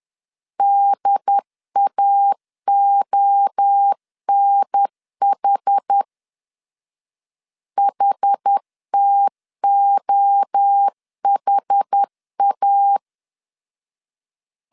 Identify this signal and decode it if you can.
Morse code, heard as a keyed tone, about 10 wpm, 800 Hz. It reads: DAONH HTOHA
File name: CW / DET / 1 3 1500C S A 10wpm 3447